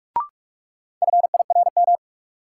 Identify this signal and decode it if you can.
Morse code, heard as a keyed tone, about 43 wpm, 700 Hz. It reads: FIRM
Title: Sentences from Top 1400 Words - New Word Drill - 45wpm